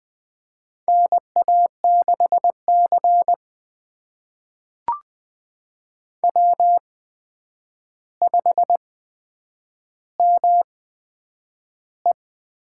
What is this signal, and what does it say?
Morse code, heard as a keyed tone, about 20 wpm, spaced as for 9 wpm, 700 Hz. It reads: NA6C W5ME
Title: USA Call Signs - 4 Characters - 8x Character Spacing - 20wpm